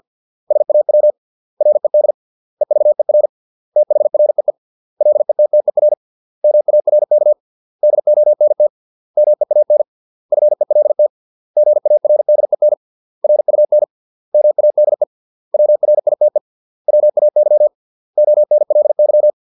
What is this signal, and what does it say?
Morse code, heard as a keyed tone, about 50 wpm, 600 Hz. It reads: SAW RED EVER THREE LETTER MARK DONT MEAN FELT GARDEN RUN MADE WRITE WAY ONLY
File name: 632 words / 632 Words 50WPM